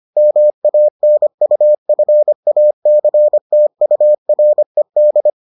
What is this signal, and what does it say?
Morse code, heard as a keyed tone, about 25 wpm, 600 Hz. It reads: MANUFACTURED